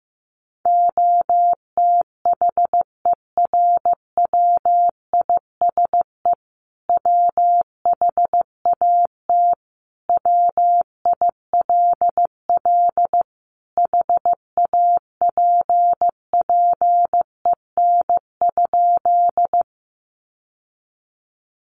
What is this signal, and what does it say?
Morse code, heard as a keyed tone, about 15 wpm, 700 Hz. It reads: OTHERWISE WHAT WILL HAPPEN?
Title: Sentences from Top 1200 Words 15wpm